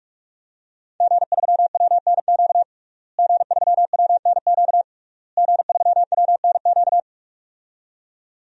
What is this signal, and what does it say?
Morse code, heard as a keyed tone, about 45 wpm, 700 Hz. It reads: G3WNQ G3WNQ G3WNQ
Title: DXCC-100 Call Signs - 5 Characters - Repeated 3x - 45wpm